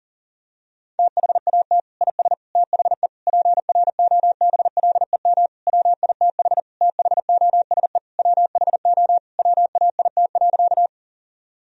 Morse code, heard as a keyed tone, about 40 wpm, 700 Hz. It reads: THAT IS THE PROBLEM WITH THOSE WHO WAIT.
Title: Sentences from Top 1000 Words - Review Part 2 - 40wpm